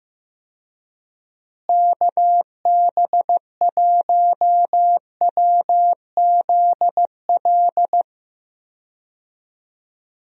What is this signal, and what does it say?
Morse code, heard as a keyed tone, about 15 wpm, 700 Hz. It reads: KB1WZL
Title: USA Call Signs - 6 Characters - Repeated 2x - 15wpm